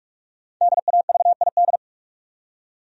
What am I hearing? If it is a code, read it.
Morse code, heard as a keyed tone, about 44 wpm, 700 Hz. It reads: DAVID
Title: USA Common Names - Max 5 Letters - 45wpm